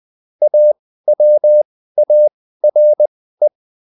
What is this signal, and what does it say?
Morse code, heard as a keyed tone, about 20 wpm, spaced as for 14 wpm, 600 Hz. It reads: AWARE